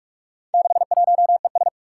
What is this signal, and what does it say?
Morse code, heard as a keyed tone, about 44 wpm, 700 Hz. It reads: 61ES
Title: Four Challenging Letter-Number Combinations 45wpm